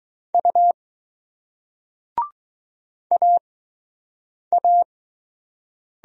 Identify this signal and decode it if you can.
Morse code, heard as a keyed tone, about 20 wpm, spaced as for 7 wpm, 700 Hz. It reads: U AA